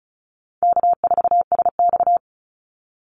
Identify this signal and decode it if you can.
Morse code, heard as a keyed tone, about 34 wpm, 700 Hz. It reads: K4SX
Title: Call Signs - 35wpm